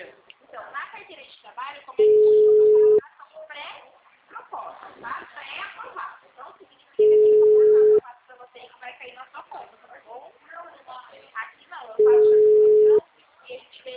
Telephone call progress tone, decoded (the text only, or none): ringback tone